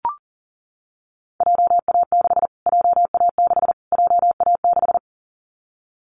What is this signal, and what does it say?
Morse code, heard as a keyed tone, about 40 wpm, 700 Hz. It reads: JA6 JA6 JA6
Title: Callsign Prefixes for Top 25 DXCC - Repeated 3x - 40wpm